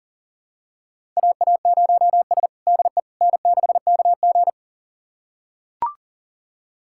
Morse code, heard as a keyed tone, about 37 wpm, 700 Hz. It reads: AA0S DE N6KG
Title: USA Callsign DE Callsign - 4 Characters - 40wpm